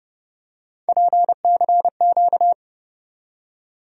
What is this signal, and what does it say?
Morse code, heard as a keyed tone, about 31 wpm, 700 Hz. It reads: PCQ
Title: USA Call Sign Postfixes - 3 Letters 30wpm